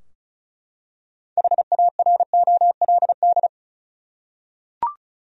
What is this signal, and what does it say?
Morse code, heard as a keyed tone, about 34 wpm, 700 Hz. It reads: HAROLD
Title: USA Common Names - 35wpm